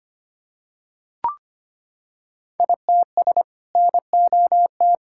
Morse code, heard as a keyed tone, about 25 wpm, 700 Hz. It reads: ITS NOT
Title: Sentences from Top 1000 Words - Review Part 5 - 25wpm